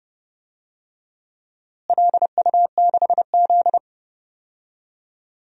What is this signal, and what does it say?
Morse code, heard as a keyed tone, about 32 wpm, 700 Hz. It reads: LU6Z
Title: DXCC-100 Call Signs - 4 Characters - Repeated 2x - 30wpm